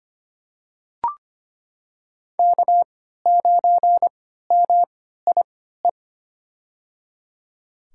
Morse code, heard as a keyed tone, about 25 wpm, spaced as for 14 wpm, 700 Hz. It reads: K9MIE
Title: USA Call Signs - 5 Characters - 3x Character Spacing - 25wpm